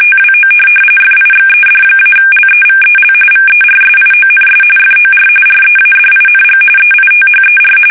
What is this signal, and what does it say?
RTTY, about 45.45 baud. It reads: TWO FIXED-LEVEL VERSIONS FOR READING EITHER
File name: fsk850 / teletype-corp